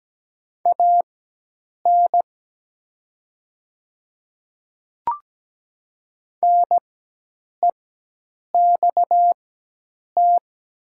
Morse code, heard as a keyed tone, about 17 wpm, spaced as for 8 wpm, 700 Hz. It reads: AN NEXT
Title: 4x Character Spacing - Top 300 Words - 17wpm